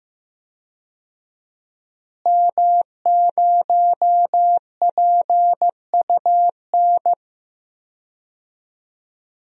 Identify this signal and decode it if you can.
Morse code, heard as a keyed tone, about 15 wpm, 700 Hz. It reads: M0PUN
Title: DXCC-100 Call Signs - 5 Characters - Repeated 2x - 15wpm